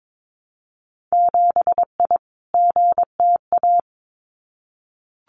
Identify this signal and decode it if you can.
Morse code, heard as a keyed tone, about 22 wpm, 700 Hz. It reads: 7I GTA
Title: Winter Field Day Exchanges Encore 22wpm